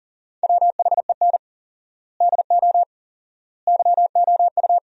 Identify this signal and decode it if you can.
Morse code, heard as a keyed tone, about 40 wpm, 700 Hz. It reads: WHEN DO YOU
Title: 4x Word-Spacing - Sentences from Top 400 Words - 40wpm